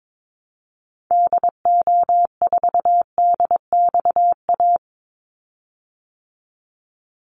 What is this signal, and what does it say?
Morse code, heard as a keyed tone, about 22 wpm, 700 Hz. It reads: DO4DXA
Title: Call Signs Repeated 3x - 22wpm